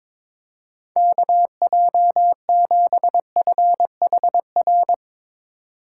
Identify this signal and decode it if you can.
Morse code, heard as a keyed tone, about 22 wpm, 700 Hz. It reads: KJ7FHR
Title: USA Call Signs - 6 Characters - 2x Character Spacing - 22wpm